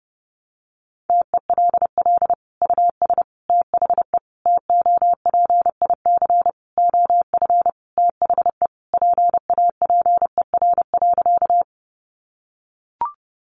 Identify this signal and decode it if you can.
Morse code, heard as a keyed tone, about 30 wpm, 700 Hz. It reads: TELL US THE TOPIC OF THE PAPER.
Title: Sentences from Top 1200 Words 30wpm